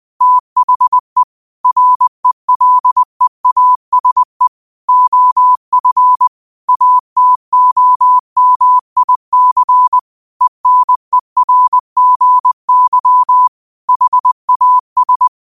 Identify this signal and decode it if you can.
Morse code, heard as a keyed tone, about 20 wpm, 1000 Hz. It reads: THE RELEASE OF ATOMIC ENERGY HAS
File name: qotdfiles / 20wpm